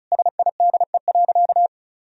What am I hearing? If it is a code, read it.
Morse code, heard as a keyed tone, about 35 wpm, 700 Hz. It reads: SIDE.
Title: Sentences from Top 1000 Words - Review Part 1 - 35wpm